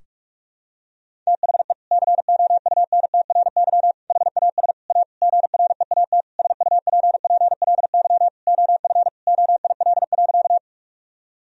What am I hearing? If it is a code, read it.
Morse code, heard as a keyed tone, about 41 wpm, 700 Hz. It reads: THE COUNTRY HAS A GREAT SUPPLY OF OIL.
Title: Sentences from Top 1000 Words - Review Part 4 - 45wpm